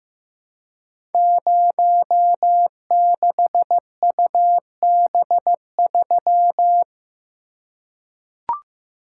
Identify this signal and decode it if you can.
Morse code, heard as a keyed tone, about 15 wpm, 700 Hz. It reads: 06UB3
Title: Code Groups - 5 Letter:Numbers - 15wpm